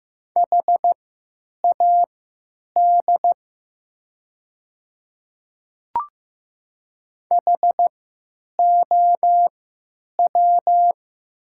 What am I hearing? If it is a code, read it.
Morse code, heard as a keyed tone, about 15 wpm, spaced as for 8 wpm, 700 Hz. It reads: HAD HOW
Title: 3x Character Spacing - Top 100 Words - 15wpm